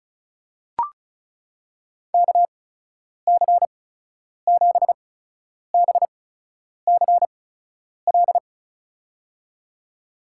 Morse code, heard as a keyed tone, about 36 wpm, spaced as for 10 wpm, 700 Hz. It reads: KC7BCL
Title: USA Call Signs - 6 Characters - 8x Character Spacing - 35wpm